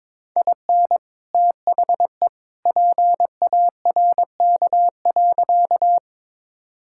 Morse code, heard as a keyed tone, about 22 wpm, 700 Hz. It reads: IN THE PARK.
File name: practice / 22wpm / Sentences with 2-Word Phrases - 22wpm